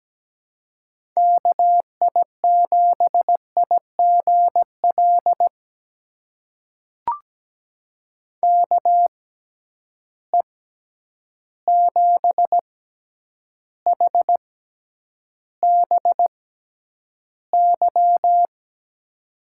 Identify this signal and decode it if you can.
Morse code, heard as a keyed tone, about 17 wpm, spaced as for 9 wpm, 700 Hz. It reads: KI7IGL KE7HBY
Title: USA Call Signs - 6 Characters - 6x Character Spacing - 17wpm